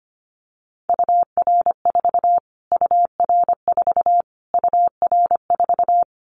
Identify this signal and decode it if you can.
Morse code, heard as a keyed tone, about 25 wpm, 700 Hz. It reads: UR4 UR4 UR4
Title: Callsign Prefixes for Top 25 DXCC - Repeated 3x - 25wpm